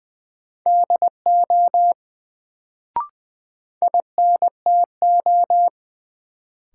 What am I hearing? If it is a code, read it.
Morse code, heard as a keyed tone, about 20 wpm, 700 Hz. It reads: DO INTO
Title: Sets of 2 Words - New Word Drill - 20wpm